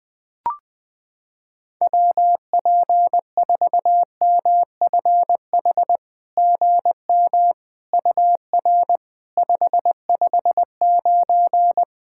Morse code, heard as a keyed tone, about 20 wpm, 700 Hz. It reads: WP4MFH GM UR 559
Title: SOTA QSO Practice 20wpm